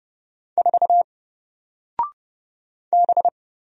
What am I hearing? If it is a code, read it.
Morse code, heard as a keyed tone, about 30 wpm, 700 Hz. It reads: 4 B
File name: Single-Letter-Number-30wpm